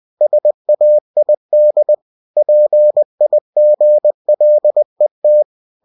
Morse code, heard as a keyed tone, about 20 wpm, 600 Hz. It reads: SAID PIGLET
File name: cw-books / winnie-the-pooh / winnie-20wpm-page-0042